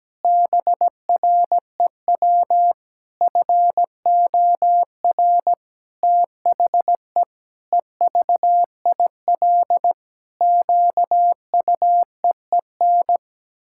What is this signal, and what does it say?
Morse code, heard as a keyed tone, about 17 wpm, 700 Hz. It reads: BREW FOR THE EVIL QUEEN
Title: Pangram Challenge v2 - 17wpm